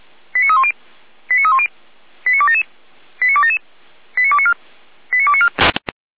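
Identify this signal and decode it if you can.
Five-tone selective call (ZVEI), heard as quick runs of five tones, tones of about 70 ms. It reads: 89310 89310 89277 89277 89294 89294
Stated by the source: ZVEI five-tone selective call, code 89276, received by SDR, from Feuerwehr Orlamünde.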